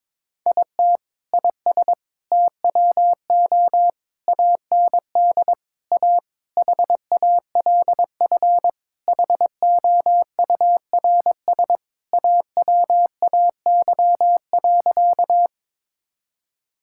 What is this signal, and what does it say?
Morse code, heard as a keyed tone, about 22 wpm, 700 Hz. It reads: IT IS TWO AND A HALF HOURS AWAY.
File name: practice / 22wpm / Sentences with 4-Word Phrases - 22wpm